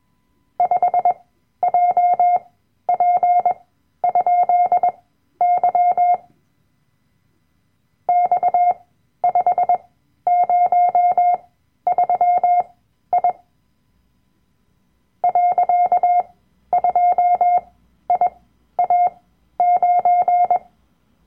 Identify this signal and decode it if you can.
Morse code, heard as a keyed tone, about 21 wpm, spaced as for 12 wpm, 710 Hz. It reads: 5JP?Y X503I .2IA9
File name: Workshop Dateien / CW-Online-Kurs / Lektion 010